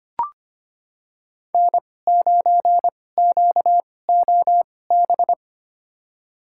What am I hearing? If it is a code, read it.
Morse code, heard as a keyed tone, about 25 wpm, spaced as for 18 wpm, 700 Hz. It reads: N9QOB
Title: USA Call Signs - 5 Characters - 2x Character Spacing - 25wpm